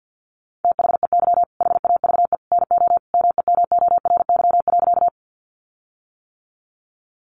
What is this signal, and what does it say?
Morse code, heard as a keyed tone, about 50 wpm, 700 Hz. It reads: THEY HAVE NO MEMORY.
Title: Sentences from Top 1300 Words 50wpm